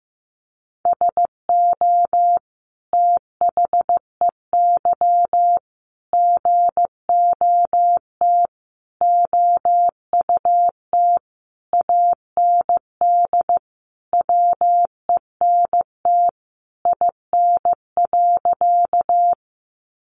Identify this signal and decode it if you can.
Morse code, heard as a keyed tone, about 15 wpm, 700 Hz. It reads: SO THEY GOT OUT AND WENT IN.
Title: Sentences from Top 300 Words 15wpm